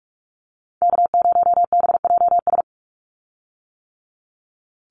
Morse code, heard as a keyed tone, about 44 wpm, 700 Hz. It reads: K0BJS
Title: USA Call Signs 45wpm